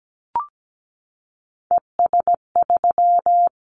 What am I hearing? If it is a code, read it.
Morse code, heard as a keyed tone, about 17 wpm, 700 Hz. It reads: ES3
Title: Callsign Prefixes for Top 100 DXCC - 17wpm